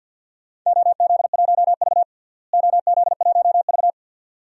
Morse code, heard as a keyed tone, about 50 wpm, 700 Hz. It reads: OZ1V OZ1V
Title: DXCC-100 Call Signs - 4 Characters - Repeated 2x - 50wpm